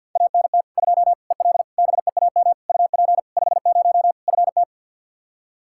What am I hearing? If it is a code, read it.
Morse code, heard as a keyed tone, about 52 wpm, 700 Hz. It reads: ANT 2 EL BEAM UP 50 FT
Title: Practice QSO - Exchange 2 - Info - 50wpm